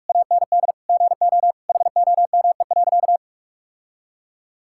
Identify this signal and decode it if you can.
Morse code, heard as a keyed tone, about 46 wpm, 700 Hz. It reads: AND GO HOME.
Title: Sentences from Top 1000 Words - Review Part 6 - 45wpm